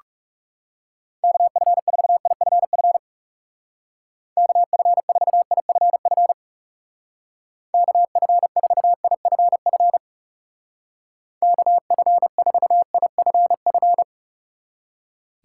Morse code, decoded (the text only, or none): KF4IFF KF4IFF KF4IFF KF4IFF